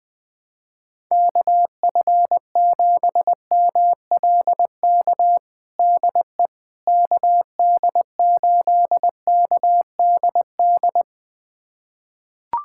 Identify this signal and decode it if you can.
Morse code, heard as a keyed tone, about 20 wpm, 700 Hz. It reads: KF7MLK DE KD8KDD
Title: USA Callsign DE Callsign 20wpm